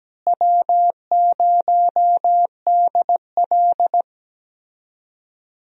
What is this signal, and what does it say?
Morse code, heard as a keyed tone, about 17 wpm, 700 Hz. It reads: W0DL
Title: USA Call Signs - 4 Characters - Repeated 2x - 17wpm